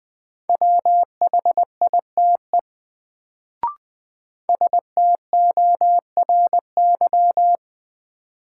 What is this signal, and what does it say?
Morse code, heard as a keyed tone, about 20 wpm, 700 Hz. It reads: WHITE STORY
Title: Sentences from Top 300 Words - New Word Drill - 20wpm